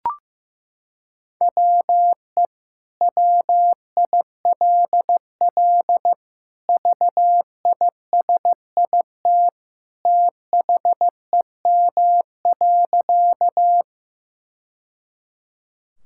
Morse code, heard as a keyed tone, about 15 wpm, 700 Hz. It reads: WE WILL VISIT THEM.
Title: Sentences from Top 1000 Words - Review Part 6 - 15wpm